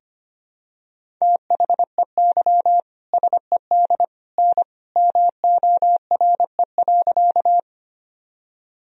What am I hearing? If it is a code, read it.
Morse code, heard as a keyed tone, about 25 wpm, 700 Hz. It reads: THEY SED N MORE.
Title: Sentences from Top 100 Words and Abbreviations 25wpm